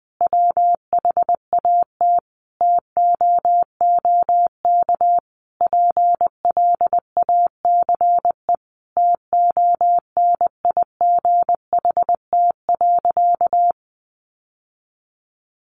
Morse code, heard as a keyed tone, about 20 wpm, 700 Hz. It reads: WHAT TOOK PLACE TONIGHT.
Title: Sentences from Top 1300 Words 20wpm